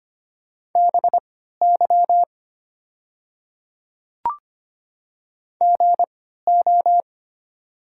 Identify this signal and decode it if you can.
Morse code, heard as a keyed tone, about 25 wpm, spaced as for 14 wpm, 700 Hz. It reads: BY GO
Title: 3x Character Spacing - Top 200 Words - 25wpm